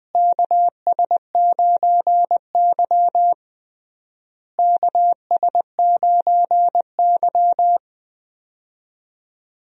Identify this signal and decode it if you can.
Morse code, heard as a keyed tone, about 21 wpm, 700 Hz. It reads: KS9Y KS9Y
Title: USA Call Signs - 4 Characters - Repeated 3x - 20wpm